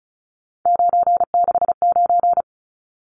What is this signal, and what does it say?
Morse code, heard as a keyed tone, about 34 wpm, 700 Hz. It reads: 969
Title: Numbers - 3-Digit 35wpm